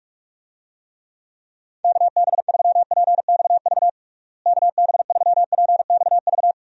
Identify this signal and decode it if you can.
Morse code, heard as a keyed tone, about 46 wpm, 700 Hz. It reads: KB3PXV KB3PXV
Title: USA Call Signs - 6 Characters - Repeated 3x - 45wpm